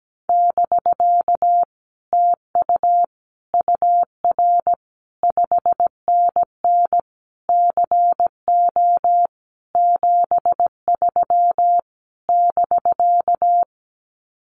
Morse code, heard as a keyed tone, about 17 wpm, 700 Hz. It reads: <BK> TU UR 5NN CO 73 <BK>
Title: Common QSO Phrases Encore 17wpm